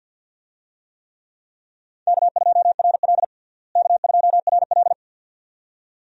Morse code, heard as a keyed tone, about 52 wpm, 700 Hz. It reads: K2RL K2RL
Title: USA Call Signs - 4 Characters - Repeated 2x - 50wpm